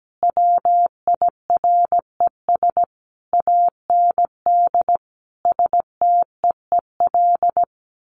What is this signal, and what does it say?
Morse code, heard as a keyed tone, about 17 wpm, 700 Hz. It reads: WIRES AND STEEL
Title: A Sound Of Thunder 17wpm